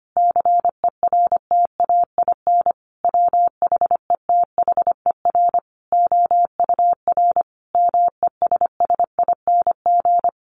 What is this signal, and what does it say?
Morse code, heard as a keyed tone, about 25 wpm, 700 Hz. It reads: CERTAIN WHETHER OUR MESSING
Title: A Sound Of Thunder 25wpm